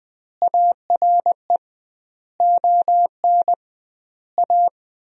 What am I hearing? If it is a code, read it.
Morse code, heard as a keyed tone, about 20 wpm, 700 Hz. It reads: ARE ON A
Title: 2x Word-Spacing - Sentences from Top 400 Words - 20wpm